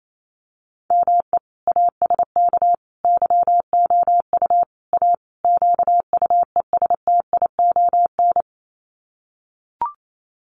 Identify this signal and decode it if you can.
Morse code, heard as a keyed tone, about 28 wpm, 700 Hz. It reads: ME ASK YOU A QUESTION
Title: Sets of 5 Words Encore2 28wpm